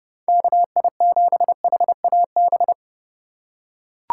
Morse code, heard as a keyed tone, about 31 wpm, 700 Hz. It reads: KI7HAB